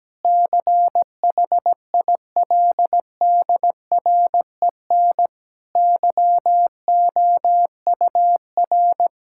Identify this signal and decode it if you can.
Morse code, heard as a keyed tone, about 17 wpm, 700 Hz. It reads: CHILDREN YOUR
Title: Sentences from Top 1000 Words - Review Part 5 - 17wpm